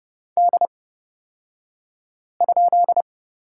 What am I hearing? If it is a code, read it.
Morse code, heard as a keyed tone, about 30 wpm, spaced as for 5 wpm, 700 Hz. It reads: D?
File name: Lesson 38d - Letter q - Two Characters 30wpm